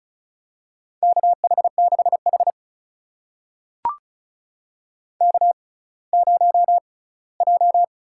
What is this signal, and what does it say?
Morse code, heard as a keyed tone, about 36 wpm, spaced as for 20 wpm, 700 Hz. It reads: KH6H K0J
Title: USA Call Signs - 4 Characters - 6x Character Spacing - 35wpm